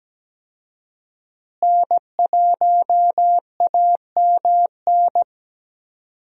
Morse code, heard as a keyed tone, about 17 wpm, 700 Hz. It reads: N1AMN
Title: USA Call Signs - 5 Characters - Repeated 3x - 17wpm